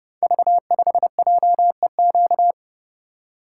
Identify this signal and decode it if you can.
Morse code, heard as a keyed tone, about 30 wpm, 700 Hz. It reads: V5JEQ
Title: Code Groups - 5 Letter:Numbers - 30wpm